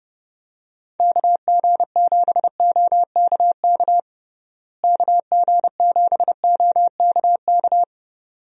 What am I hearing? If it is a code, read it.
Morse code, heard as a keyed tone, about 30 wpm, 700 Hz. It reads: KG7OKK KG7OKK
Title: USA Call Signs - 6 Characters - Repeated 2x - 30wpm